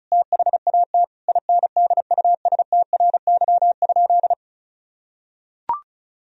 Morse code, heard as a keyed tone, about 34 wpm, 700 Hz. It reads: THAT INDUSTRY?
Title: Sentences from Top 1000 Words - Review Part 4 - 35wpm